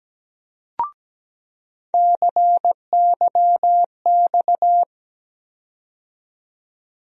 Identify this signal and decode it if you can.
Morse code, heard as a keyed tone, about 17 wpm, 700 Hz. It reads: CYX
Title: USA Call Sign Postfixes - 3 Letters 17wpm